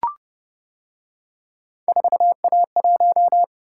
Morse code, heard as a keyed tone, about 29 wpm, 700 Hz. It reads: 4A1